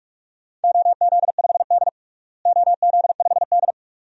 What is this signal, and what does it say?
Morse code, heard as a keyed tone, about 45 wpm, 700 Hz. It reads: OZ5D OZ5D
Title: DXCC-100 Call Signs - 4 Characters - Repeated 2x - 45wpm